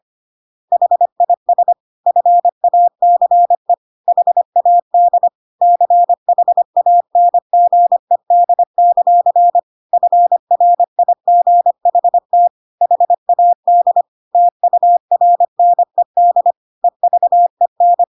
Morse code, heard as a keyed tone, about 25 wpm, 700 Hz. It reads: HIS FACE HAD CHANGED; FRIGHT HAD TURNED EVEN
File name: Pinocchio / Pinocchio ch1 25 25 ik1hge